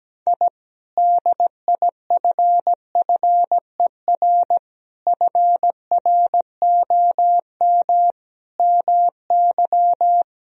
Morse code, heard as a keyed tone, about 17 wpm, 700 Hz. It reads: I DIFFER FROM MY
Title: Sentences from Top 1000 Words - Review Part 1 - 17wpm